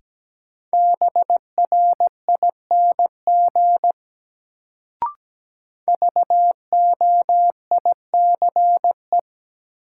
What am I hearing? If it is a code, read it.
Morse code, heard as a keyed tone, about 17 wpm, 700 Hz. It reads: BRING VOICE
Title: Sentences from Top 500 Words - New Word Drill - 17wpm